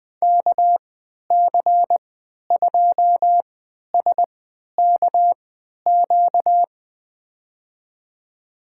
Morse code, heard as a keyed tone, about 21 wpm, spaced as for 12 wpm, 700 Hz. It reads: KC2SKQ
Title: USA Call Signs - 6 Characters - 3x Character Spacing - 20wpm